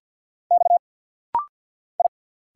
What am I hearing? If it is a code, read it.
Morse code, heard as a keyed tone, about 50 wpm, 700 Hz. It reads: X I
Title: Single Letters - Rapid-Fire 50wpm